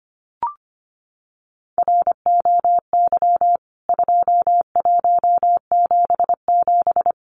Morse code, heard as a keyed tone, about 25 wpm, 700 Hz. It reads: ROY 2177
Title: CWT Exchange 25wpm